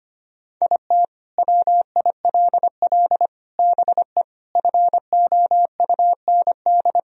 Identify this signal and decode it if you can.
Morse code, heard as a keyed tone, about 25 wpm, 700 Hz. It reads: IT WILL BE FOUND
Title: Sentences from Top 1000 Words - Review Part 1 - 25wpm